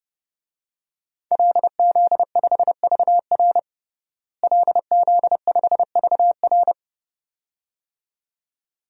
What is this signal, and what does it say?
Morse code, heard as a keyed tone, about 32 wpm, 700 Hz. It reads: LZ5VR LZ5VR
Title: DXCC-100 Call Signs - 5 Characters - Repeated 2x - 30wpm